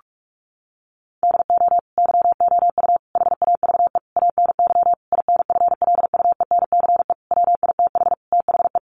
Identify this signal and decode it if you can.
Morse code, heard as a keyed tone, about 44 wpm, 700 Hz. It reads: DO YOU HAVE ANY INFLUENCE WITH THE
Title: Sentences from Top 1100 Words 45wpm